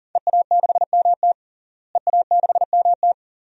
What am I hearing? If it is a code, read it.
Morse code, heard as a keyed tone, about 40 wpm, 700 Hz. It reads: EA6MT EA6MT
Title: DXCC-100 Call Signs - 5 Characters - Repeated 2x - 40wpm